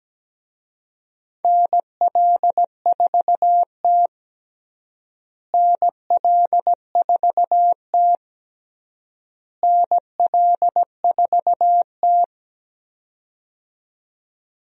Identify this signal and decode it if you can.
Morse code, heard as a keyed tone, about 17 wpm, 700 Hz. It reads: NL4T NL4T NL4T
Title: USA Call Signs - 4 Characters - Repeated 3x - 17wpm